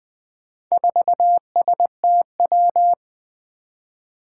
Morse code, heard as a keyed tone, about 20 wpm, 700 Hz. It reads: 4STW
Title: Four Challenging Letter-Number Combinations 20wpm